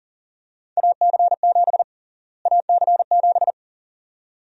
Morse code, heard as a keyed tone, about 40 wpm, 700 Hz. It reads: AC7 AC7
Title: USA Call Sign Prefixes - Repeated 2x - 40wpm